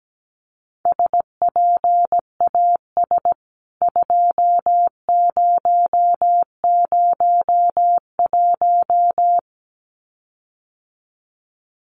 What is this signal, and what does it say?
Morse code, heard as a keyed tone, about 17 wpm, 700 Hz. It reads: SPAS 2001
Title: CWT Exchange 17wpm